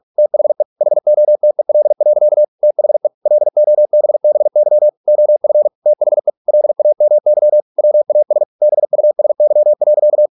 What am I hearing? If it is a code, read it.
Morse code, heard as a keyed tone, about 46 wpm, 600 Hz. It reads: THE HOTEL. THE LOBBY OF THE LAMY WAS BUSY.